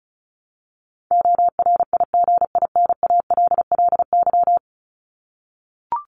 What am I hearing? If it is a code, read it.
Morse code, heard as a keyed tone, about 34 wpm, 700 Hz. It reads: ORIGINALLY
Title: LY Suffix - Top 100 Words with Base Words 35wpm